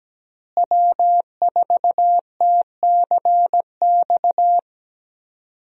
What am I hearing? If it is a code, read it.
Morse code, heard as a keyed tone, about 17 wpm, 700 Hz. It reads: W4TCX
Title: USA Call Signs - 5 Characters - Repeated 2x - 17wpm